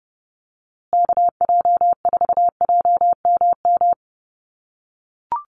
Morse code, heard as a keyed tone, about 30 wpm, 700 Hz. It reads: KJ4JMM
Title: USA Call Signs Repeated 3x 30wpm